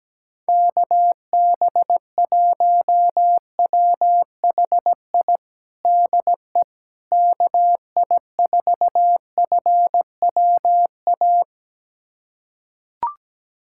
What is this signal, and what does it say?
Morse code, heard as a keyed tone, about 17 wpm, 700 Hz. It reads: KB1WHI DE KI4FWA
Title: USA Callsign DE Callsign - 6 Characters - 17wpm